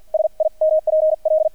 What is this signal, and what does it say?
Morse code, heard as a keyed tone, about 50 wpm, spaced as for 41 wpm, 630 Hz. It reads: SIMPL
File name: archived / CRSnet